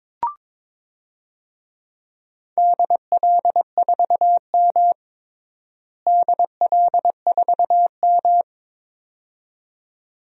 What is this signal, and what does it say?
Morse code, heard as a keyed tone, about 22 wpm, 700 Hz. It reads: DL4M DL4M
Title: DXCC-100 Call Signs - 4 Characters - Repeated 2x - 22wpm